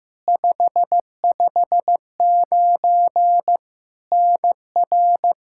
Morse code, heard as a keyed tone, about 15 wpm, 700 Hz. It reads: 559 NR
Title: Practice QSO - Exchange 1 - Beginning - 15wpm